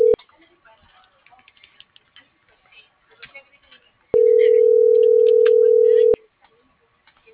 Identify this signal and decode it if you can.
Ringback tone (North America), 440 + 480 Hz.